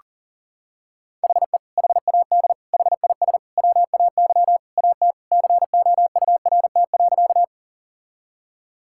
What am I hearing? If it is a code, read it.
Morse code, heard as a keyed tone, about 37 wpm, 700 Hz. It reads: HE HAD HIS WAY AT COURT.
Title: Sentences from Top 1600 Words 40wpm